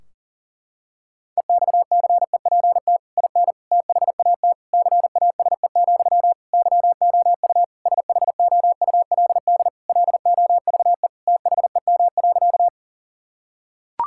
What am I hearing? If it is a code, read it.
Morse code, heard as a keyed tone, about 40 wpm, 700 Hz. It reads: EXCEPT IN THAT CASE, YOU SHOULD LOVE THEM.
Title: Sentences from Top 1600 Words 40wpm